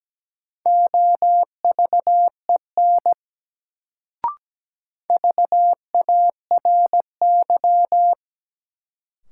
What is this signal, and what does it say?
Morse code, heard as a keyed tone, about 17 wpm, 700 Hz. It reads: OVEN VARY
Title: Sentences from Top 1200 Words - New Word Drill Alt - 17wpm